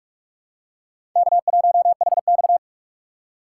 Morse code, heard as a keyed tone, about 42 wpm, 700 Hz. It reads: K1HX
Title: USA Call Signs - 4 Characters - 3x Character Spacing - 45wpm